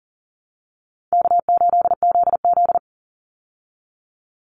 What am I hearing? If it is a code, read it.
Morse code, heard as a keyed tone, about 39 wpm, 700 Hz. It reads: K8ZZ
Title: Call Signs Repeated 3x - 40wpm